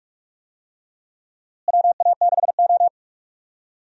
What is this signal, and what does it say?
Morse code, heard as a keyed tone, about 47 wpm, 700 Hz. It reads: WA6O
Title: USA Call Signs - 4 Characters - Repeated 2x - 45wpm